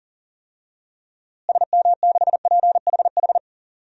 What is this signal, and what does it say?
Morse code, heard as a keyed tone, about 40 wpm, 700 Hz. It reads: SM6PHH
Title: DXCC-100 Call Signs - 6 Characters - Repeated 2x - 40wpm